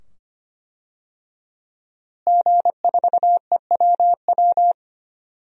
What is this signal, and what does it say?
Morse code, heard as a keyed tone, about 25 wpm, 700 Hz. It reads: G4EWW
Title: DXCC-100 Call Signs - 5 Characters - Repeated 2x - 25wpm